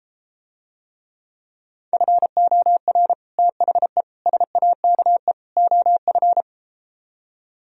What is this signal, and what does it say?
Morse code, heard as a keyed tone, about 33 wpm, 700 Hz. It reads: FOR THE SAKE OF